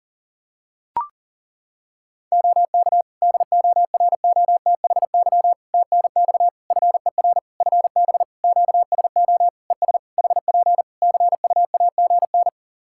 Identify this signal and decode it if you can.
Morse code, heard as a keyed tone, about 40 wpm, 700 Hz. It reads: OK DOROTHY TNX FER FB QSO ES HP CUAGN
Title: Practice QSO - Exchange 4 - Ending - 40wpm